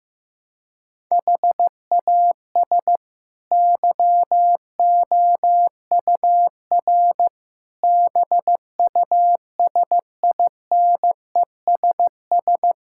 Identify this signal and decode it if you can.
Morse code, heard as a keyed tone, about 16 wpm, 700 Hz. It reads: HAS YOUR BUSINESS
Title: Sentences from Top 1000 Words - Review Part 6 - 15wpm